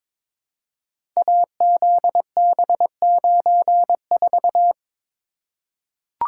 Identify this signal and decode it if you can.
Morse code, heard as a keyed tone, about 23 wpm, 700 Hz. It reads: AZB94
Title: Code Groups - 5 Letter:Numbers - 22wpm